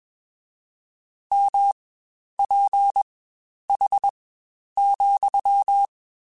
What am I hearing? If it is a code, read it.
Morse code, heard as a keyed tone, about 21 wpm, spaced as for 10 wpm, 800 Hz. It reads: MPH,
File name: Spaced Characters / Spaced Characters-mp3s / Weather 5 12 sp @20